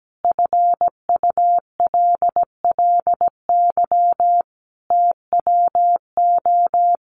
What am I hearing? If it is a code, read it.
Morse code, heard as a keyed tone, about 17 wpm, 700 Hz. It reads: FULLY TWO
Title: Sentences from Top 1300 Words 17wpm